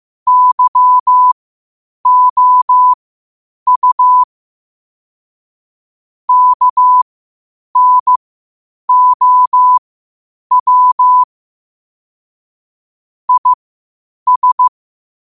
Morse code, heard as a keyed tone, about 15 wpm, spaced as for 8 wpm, 1000 Hz. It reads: YOU KNOW IS
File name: qotdfiles / 8wpm